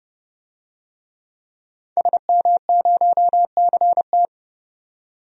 Morse code, heard as a keyed tone, about 30 wpm, 700 Hz. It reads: SM0CT
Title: DXCC-100 Call Signs - 5 Characters - 30wpm